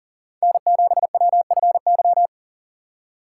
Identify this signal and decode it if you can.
Morse code, heard as a keyed tone, about 48 wpm, 700 Hz. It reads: N7WFY
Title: Speed Racing - USA 1x3 Call Signs 40wpm